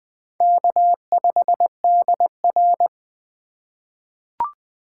Morse code, heard as a keyed tone, about 20 wpm, 700 Hz. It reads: K5DR
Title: USA Call Signs - 4 Characters - 2x Character Spacing - 20wpm